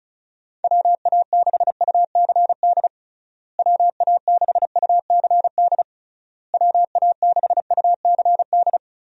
Morse code, heard as a keyed tone, about 36 wpm, 700 Hz. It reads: WA6UCD WA6UCD WA6UCD
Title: USA Call Signs - 6 Characters - Repeated 3x - 35wpm